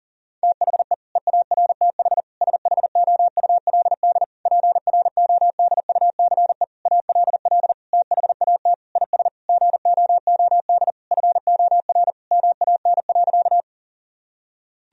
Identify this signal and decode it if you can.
Morse code, heard as a keyed tone, about 40 wpm, 700 Hz. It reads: THE EARTH SHOULD PRODUCE ALL THAT IS GOOD FOR MAN.
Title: Sentences from Top 1000 Words - Review Part 3 - 40wpm